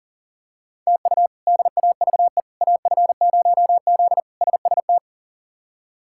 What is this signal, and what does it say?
Morse code, heard as a keyed tone, about 40 wpm, 700 Hz. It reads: TU DAVE AF0Z SST
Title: K1USN SST QSO Practice - 40wpm